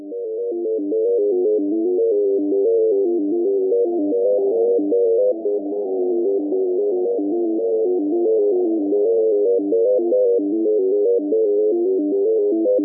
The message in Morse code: none